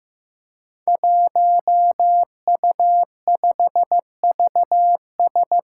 Morse code, heard as a keyed tone, about 15 wpm, 700 Hz. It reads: 1U5VS
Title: Five Challenging Letter-Number Combinations 15wpm